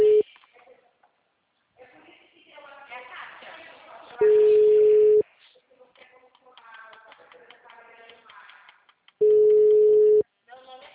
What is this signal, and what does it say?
Ringback tone (Europe, 425 Hz), with speech.
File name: out- -2004- - - .90290